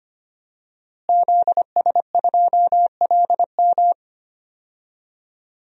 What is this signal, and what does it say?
Morse code, heard as a keyed tone, about 25 wpm, 700 Hz. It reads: ZS2LM
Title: DXCC-100 Call Signs - 5 Characters - Repeated 3x - 25wpm